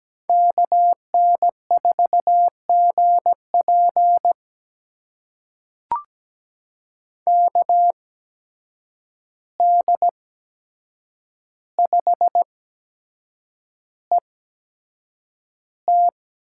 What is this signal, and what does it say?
Morse code, heard as a keyed tone, about 17 wpm, spaced as for 7 wpm, 700 Hz. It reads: KN4GP KD5ET
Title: USA Call Signs - 8x Character Spacing - 17wpm